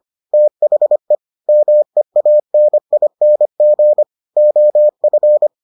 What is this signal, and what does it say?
Morse code, heard as a keyed tone, about 25 wpm, 600 Hz. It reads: THE MEANING OF